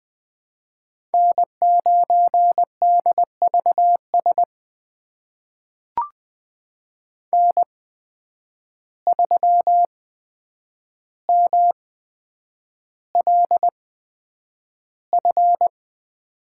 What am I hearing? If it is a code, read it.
Morse code, heard as a keyed tone, about 20 wpm, spaced as for 9 wpm, 700 Hz. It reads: N9DVS N3MLF
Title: USA Call Signs - 5 Characters - 8x Character Spacing - 20wpm